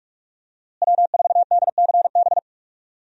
Morse code, heard as a keyed tone, about 42 wpm, 700 Hz. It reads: W4DCB
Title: USA Call Signs - 5 Characters - Repeated 3x - 45wpm